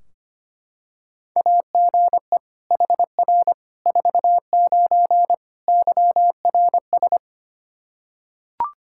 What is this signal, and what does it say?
Morse code, heard as a keyed tone, about 25 wpm, 700 Hz. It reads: AGE HR 49 YRS
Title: K1USN SST QSO Practice 25wpm